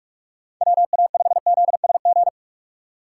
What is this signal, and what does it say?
Morse code, heard as a keyed tone, about 46 wpm, 700 Hz. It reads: WA5ZSG
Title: USA Call Signs - 6 Characters - 6x Character Spacing - 45wpm